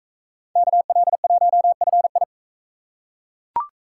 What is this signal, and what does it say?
Morse code, heard as a keyed tone, about 38 wpm, 700 Hz. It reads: KL1FI